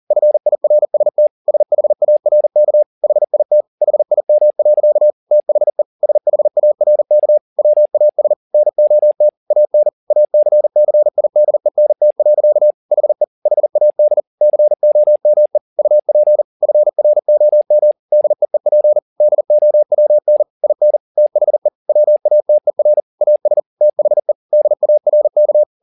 Morse code, heard as a keyed tone, about 40 wpm, 600 Hz. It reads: FIRST SHARK HIT HIM. THE SHARK WAS NOT AN ACCIDENT. HE HAD COME UP FROM DEEP DOWN IN THE WATER AS THE DARK
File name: the old man and the sea 2 40wpm